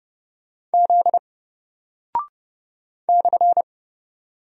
Morse code, heard as a keyed tone, about 30 wpm, 700 Hz. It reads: Z /
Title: Single Character (Letter:Number:Punctuation:Prosign) Alt 30wpm